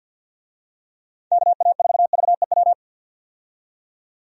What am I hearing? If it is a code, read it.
Morse code, heard as a keyed tone, about 52 wpm, 700 Hz. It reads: KA4VEW
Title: USA Call Signs - 6 Characters - 50wpm